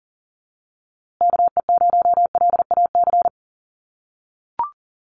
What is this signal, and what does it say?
Morse code, heard as a keyed tone, about 40 wpm, 700 Hz. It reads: KE0LAC
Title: USA Call Signs Repeated 3x 40wpm